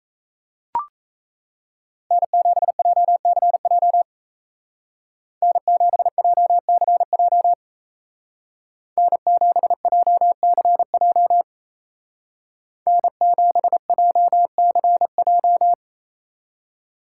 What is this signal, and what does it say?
Morse code, heard as a keyed tone, about 36 wpm, 700 Hz. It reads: N7JCJ N7JCJ N7JCJ N7JCJ